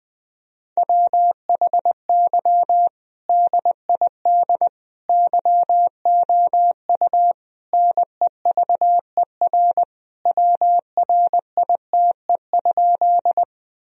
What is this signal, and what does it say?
Morse code, heard as a keyed tone, about 20 wpm, 700 Hz. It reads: WHY DID YOU NEVER WRITE?
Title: Sentences from Top 1000 Words - Review Part 1 - 20wpm